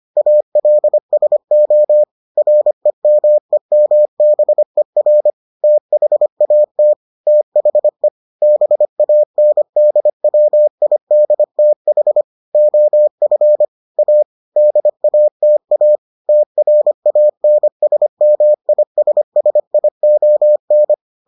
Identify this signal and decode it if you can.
Morse code, heard as a keyed tone, about 25 wpm, 600 Hz. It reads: ALSO REMEMBER THAT THE BANDWIDTH OF A DATA TRANSMISSION